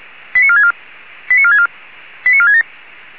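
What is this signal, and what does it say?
Five-tone selective call (ZVEI), heard as quick runs of five tones, tones of about 70 ms. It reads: 89464 89464 89467
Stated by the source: ZVEI five-tone selective call received on SDR, from the server of Feuerwehr Orlamünde (fire brigade).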